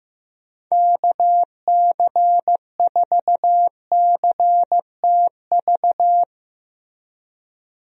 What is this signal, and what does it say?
Morse code, heard as a keyed tone, about 15 wpm, 700 Hz. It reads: KC4CTV
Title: USA Call Signs - 6 Characters - Repeated 2x - 15wpm